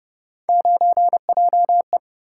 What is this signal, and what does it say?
Morse code, heard as a keyed tone, about 29 wpm, 700 Hz. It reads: 9JE